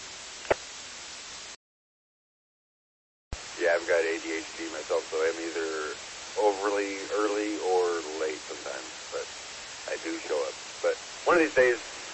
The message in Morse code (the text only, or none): none